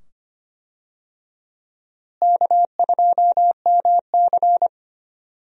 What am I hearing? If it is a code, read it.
Morse code, heard as a keyed tone, about 25 wpm, 700 Hz. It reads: K2MC
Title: USA Call Signs - 4 Characters - Repeated 2x - 25wpm